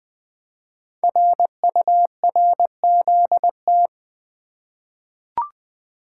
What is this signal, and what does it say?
Morse code, heard as a keyed tone, about 20 wpm, 700 Hz. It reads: RURZT
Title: Code Groups - 5 Letters - 20wpm